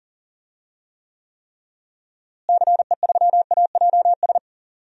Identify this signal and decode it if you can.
Morse code, heard as a keyed tone, about 40 wpm, 700 Hz. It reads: CE3AJS
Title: DXCC-100 Call Signs - 6 Characters - Repeated 3x - 40wpm